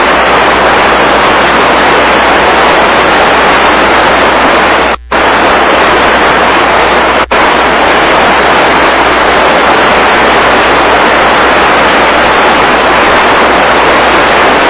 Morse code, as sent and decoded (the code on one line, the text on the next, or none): none